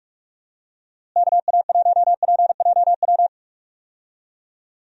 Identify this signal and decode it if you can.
Morse code, heard as a keyed tone, about 46 wpm, 700 Hz. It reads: KA1PJW
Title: USA Call Signs - 6 Characters - Repeated 3x - 45wpm